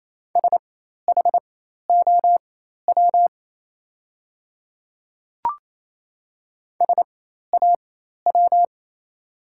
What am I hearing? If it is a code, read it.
Morse code, heard as a keyed tone, about 27 wpm, spaced as for 13 wpm, 690 Hz. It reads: SHOW SAW
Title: 4x Character Spacing - Top 500 Words - 28wpm